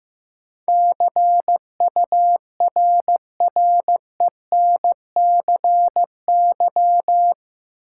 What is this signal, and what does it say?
Morse code, heard as a keyed tone, about 16 wpm, 700 Hz. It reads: CURRENCY
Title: Sentences from Top 1500 Words - New Word Drill - 15wpm